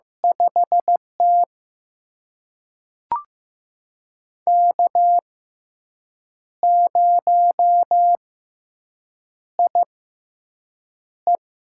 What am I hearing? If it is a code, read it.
Morse code, heard as a keyed tone, about 15 wpm, spaced as for 6 wpm, 700 Hz. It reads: 5T K0IE
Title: USA Call Signs - 4 Characters - 6x Character Spacing - 15wpm